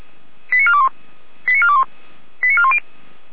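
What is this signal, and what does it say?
Five-tone selective call (ZVEI), heard as quick runs of five tones, tones of about 70 ms. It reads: 89421 89421 89410
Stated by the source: ZVEI five-tone selective call received on SDR.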